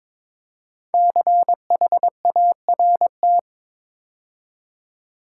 Morse code, heard as a keyed tone, about 22 wpm, 700 Hz. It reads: CHART
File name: 5 Letter Words - First 3 Letters Given - 22wpm